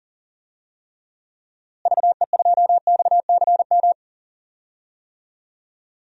Morse code, heard as a keyed tone, about 37 wpm, 700 Hz. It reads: VE2XCM
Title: DXCC-100 Call Signs - 6 Characters - Repeated 2x - 40wpm